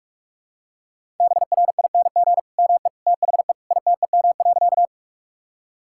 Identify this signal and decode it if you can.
Morse code, heard as a keyed tone, about 46 wpm, 700 Hz. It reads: BRING ME THE ITEM.
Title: Sentences from Top 1000 Words - Review Part 6 - 45wpm